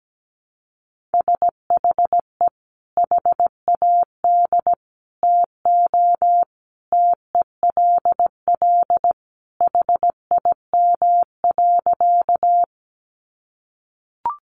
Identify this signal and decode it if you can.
Morse code, heard as a keyed tone, about 17 wpm, 700 Hz. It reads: SHE HAD TO TELL HIM.
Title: Sentences from Top 200 Words 17wpm